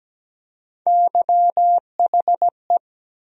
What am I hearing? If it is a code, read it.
Morse code, heard as a keyed tone, about 17 wpm, 700 Hz. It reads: YHE
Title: USA Call Sign Postfixes - 3 Letters 17wpm